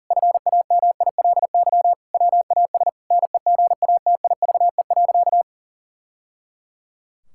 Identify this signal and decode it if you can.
Morse code, heard as a keyed tone, about 37 wpm, 700 Hz. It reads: FAMILY WAS NEGATIVE.
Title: Sentences from Top 1000 Words - Review Part 6 - 40wpm